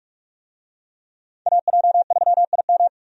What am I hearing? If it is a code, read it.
Morse code, heard as a keyed tone, about 45 wpm, 700 Hz. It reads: AJ3IM
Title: USA Call Signs - 5 Characters - 45wpm